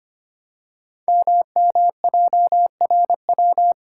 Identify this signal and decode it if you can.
Morse code, heard as a keyed tone, about 25 wpm, 700 Hz. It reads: MMJRW
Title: Code Groups - 5 Letters - 25wpm